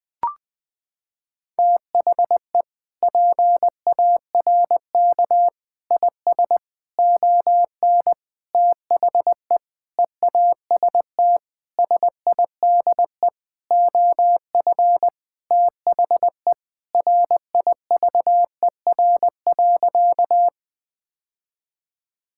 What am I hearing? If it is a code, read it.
Morse code, heard as a keyed tone, about 21 wpm, 700 Hz. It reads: THE PARK IS ON THE EAST SIDE OF THE RIVER.
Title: Sentences with 6-Word Phrases - 20wpm